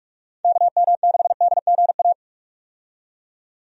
Morse code, heard as a keyed tone, about 44 wpm, 700 Hz. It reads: KM6DGA
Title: USA Call Signs - 6 Characters - 45wpm